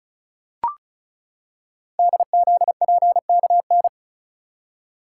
Morse code, heard as a keyed tone, about 36 wpm, 700 Hz. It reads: DZPKN
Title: Code Groups - 5 Letters - 35wpm